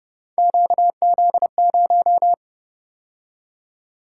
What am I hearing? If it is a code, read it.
Morse code, heard as a keyed tone, about 31 wpm, 700 Hz. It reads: QZ0